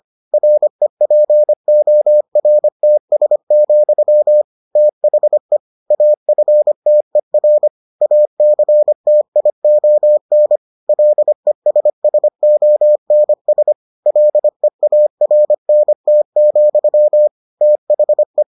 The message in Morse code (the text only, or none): REPORTS, THE AFTER ACTION LESSONS LEARNT, THE